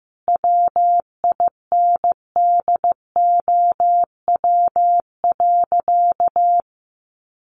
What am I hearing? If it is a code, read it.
Morse code, heard as a keyed tone, about 15 wpm, 700 Hz. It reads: WINDOW.
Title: Additional Easy Phrases 15wpm